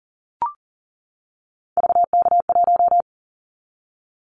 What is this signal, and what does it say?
Morse code, heard as a keyed tone, about 40 wpm, 700 Hz. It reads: VK1
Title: Callsign Prefixes for Top 50 DXCC - 40wpm